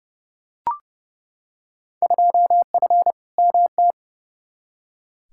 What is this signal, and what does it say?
Morse code, heard as a keyed tone, about 30 wpm, 700 Hz. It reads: 2F MT
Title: Field Day Exchanges Encore - 30wpm